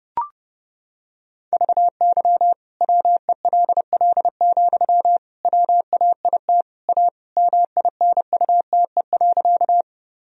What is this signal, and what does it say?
Morse code, heard as a keyed tone, about 28 wpm, 700 Hz. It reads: VY WELL, WAIT A MINUTE.
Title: Sentences from Top 500 Words and Abbreviations 30wpm